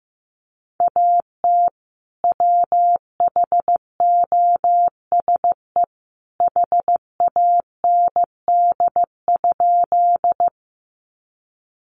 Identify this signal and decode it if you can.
Morse code, heard as a keyed tone, about 15 wpm, 700 Hz. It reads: AT WHOSE HAND?
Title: Sentences from Top 1200 Words 15wpm